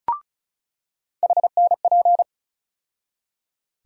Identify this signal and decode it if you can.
Morse code, heard as a keyed tone, about 34 wpm, 700 Hz. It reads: HNP
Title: USA Call Sign Postfixes - 3 Letters 35wpm